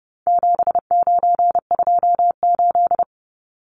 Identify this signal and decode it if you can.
Morse code, heard as a keyed tone, about 30 wpm, 700 Hz. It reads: 7928
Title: Numbers - 4-Digit 30wpm